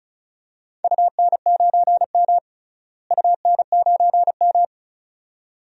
Morse code, heard as a keyed tone, about 34 wpm, 700 Hz. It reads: UN9M UN9M
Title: DXCC-100 Call Signs - 4 Characters - Repeated 2x - 35wpm